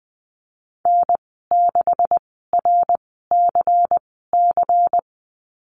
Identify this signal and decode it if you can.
Morse code, heard as a keyed tone, about 20 wpm, spaced as for 14 wpm, 700 Hz. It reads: N6RCC
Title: 128 Lesson 21g - Number 6 - Call Signs 20wpm